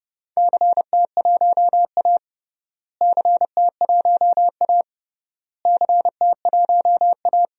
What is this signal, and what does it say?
Morse code, heard as a keyed tone, about 32 wpm, 700 Hz. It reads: CT1A CT1A CT1A
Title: DXCC-100 Call Signs - 4 Characters - Repeated 3x - 30wpm